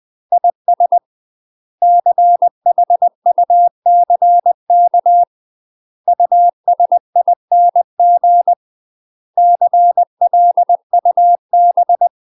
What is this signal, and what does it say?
Morse code, heard as a keyed tone, about 20 wpm, 700 Hz. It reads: IS CHUCK USING CLUB
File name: UK MASTER QSOs TRAINING / 20WPM / UK MASTER QSOs 0073